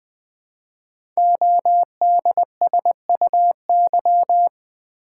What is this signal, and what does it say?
Morse code, heard as a keyed tone, about 20 wpm, 700 Hz. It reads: ODSUY
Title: Code Groups - 5 Letters - 20wpm